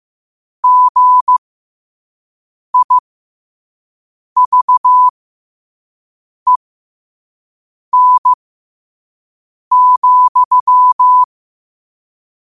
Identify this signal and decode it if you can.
Morse code, heard as a keyed tone, about 15 wpm, spaced as for 5 wpm, 1000 Hz. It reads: GIVEN,
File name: qotdfiles / 5wpm